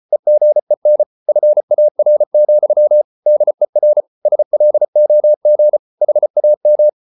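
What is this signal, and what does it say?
Morse code, heard as a keyed tone, about 34 wpm, 600 Hz. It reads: EGEN FAR, DER SLOG HAM
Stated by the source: Morse code at 33 wpm, QRQ.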